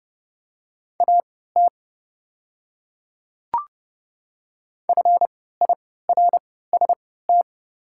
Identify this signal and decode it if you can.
Morse code, heard as a keyed tone, about 30 wpm, spaced as for 17 wpm, 700 Hz. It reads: AT FIRST
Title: 3x Character Spacing - Top 200 Words - 30wpm